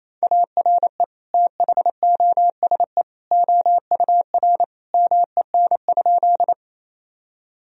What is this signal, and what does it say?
Morse code, heard as a keyed tone, about 28 wpm, 700 Hz. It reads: ARE THOSE OUR MEN?
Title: Sentences from Top 1000 Words - Review Part 1 - 28wpm